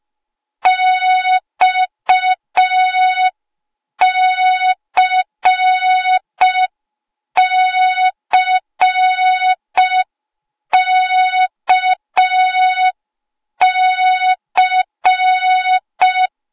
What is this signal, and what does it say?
Morse code, heard as a keyed tone, about 5 wpm, 3700 Hz. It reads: XCCKC
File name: Morze-HA2MN / hang / 08KXC HA2MN